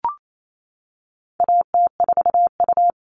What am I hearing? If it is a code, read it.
Morse code, heard as a keyed tone, about 28 wpm, 700 Hz. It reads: AT4U
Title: All Cut Numbers - 4-Digit 28wpm